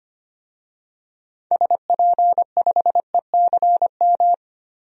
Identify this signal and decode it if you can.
Morse code, heard as a keyed tone, about 25 wpm, 700 Hz. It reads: SP5ECM
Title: DXCC-100 Call Signs - 6 Characters - 25wpm